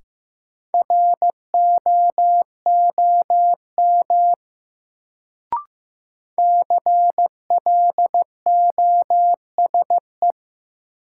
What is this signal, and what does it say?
Morse code, heard as a keyed tone, about 15 wpm, 700 Hz. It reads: ROOM CLOSE
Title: Sentences from Top 300 Words - New Word Drill - 15wpm